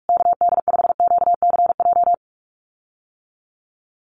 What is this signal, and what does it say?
Morse code, heard as a keyed tone, about 46 wpm, 690 Hz. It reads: KD5QCJ
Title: USA Call Signs Repeated 3x 45wpm